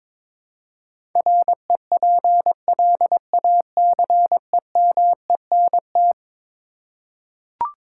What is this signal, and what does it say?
Morse code, heard as a keyed tone, about 22 wpm, 700 Hz. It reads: REPLACEMENT
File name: practice / 22wpm / Eleven Letter Words - 22wpm